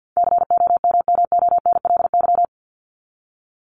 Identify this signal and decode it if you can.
Morse code, heard as a keyed tone, about 50 wpm, 700 Hz. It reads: COMMONLY
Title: LY Suffix - Top 100 Words with Base Words 50wpm